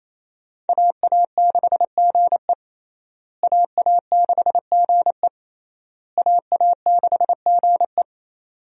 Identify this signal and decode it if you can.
Morse code, heard as a keyed tone, about 28 wpm, 700 Hz. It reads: AA6GE AA6GE AA6GE
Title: USA Call Signs - 5 Characters - Repeated 3x - 28wpm